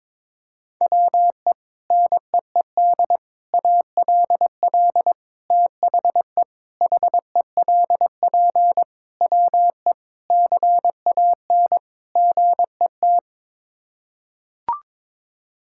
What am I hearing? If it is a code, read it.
Morse code, heard as a keyed tone, about 22 wpm, 700 Hz. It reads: WE NEED ALL THE HELP WE CAN GET
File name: practice / 22wpm / Sets of 8 Words - 22wpm